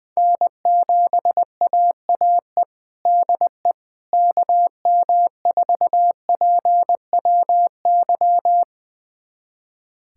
Morse code, heard as a keyed tone, about 20 wpm, 700 Hz. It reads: N7AAE DE KM4PWY
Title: USA Callsign DE Callsign 20wpm